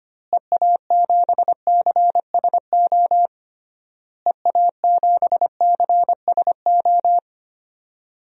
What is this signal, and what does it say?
Morse code, heard as a keyed tone, about 25 wpm, 700 Hz. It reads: EA7CSO EA7CSO
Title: DXCC-100 Call Signs - 6 Characters - Repeated 3x - 25wpm